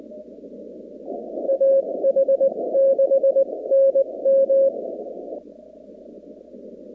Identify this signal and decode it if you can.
Morse code, heard as a keyed tone, about 20 wpm, spaced as for 17 wpm, 540 Hz. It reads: AH6NM
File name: 2019-cw-wpx-kt5j-k5pi